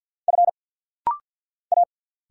Morse code, heard as a keyed tone, about 50 wpm, 690 Hz. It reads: F A
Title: Single Letters - Rapid-Fire 50wpm